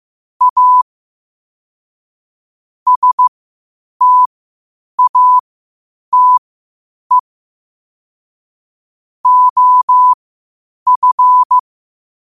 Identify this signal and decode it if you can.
Morse code, heard as a keyed tone, about 15 wpm, spaced as for 8 wpm, 1000 Hz. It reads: A STATE OF
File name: qotdfiles / 8wpm